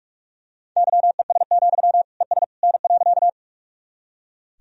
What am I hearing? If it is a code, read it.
Morse code, heard as a keyed tone, about 46 wpm, 700 Hz. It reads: YES, ES N.
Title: Sentences from Top 500 Words and Abbreviations 45wpm